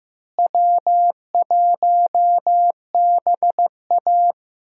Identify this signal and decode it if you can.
Morse code, heard as a keyed tone, about 15 wpm, 700 Hz. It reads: W1BA
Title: USA Call Signs - 4 Characters - Repeated 2x - 15wpm